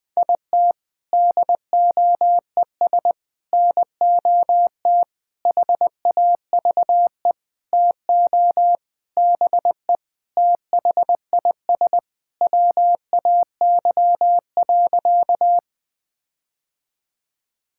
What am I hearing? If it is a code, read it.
Morse code, heard as a keyed tone, about 19 wpm, 700 Hz. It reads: IT DOES NOT HAVE TO BE THIS WAY.
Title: Sentences with 5-Word Phrases - 20wpm